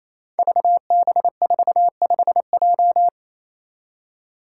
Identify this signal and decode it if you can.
Morse code, heard as a keyed tone, about 28 wpm, 700 Hz. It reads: VB45J
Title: Code Groups - 5 Letter:Numbers - 28wpm